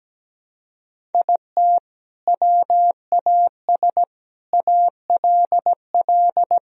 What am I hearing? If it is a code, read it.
Morse code, heard as a keyed tone, about 17 wpm, 700 Hz. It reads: IT WAS ALL
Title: Sentences from Top 1000 Words - Review Part 1 - 17wpm